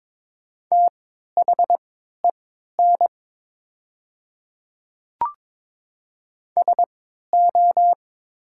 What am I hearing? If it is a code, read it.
Morse code, heard as a keyed tone, about 22 wpm, spaced as for 12 wpm, 700 Hz. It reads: THEN SO
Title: 3x Character Spacing - Top 100 Words - 22wpm